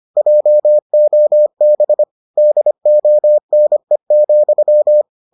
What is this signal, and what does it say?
Morse code, heard as a keyed tone, about 25 wpm, 600 Hz. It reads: JOB DONE,